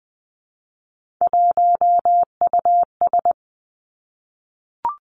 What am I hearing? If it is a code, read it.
Morse code, heard as a keyed tone, about 20 wpm, 700 Hz. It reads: 1US
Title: Three Challenging Letter-Number Combinations 20wpm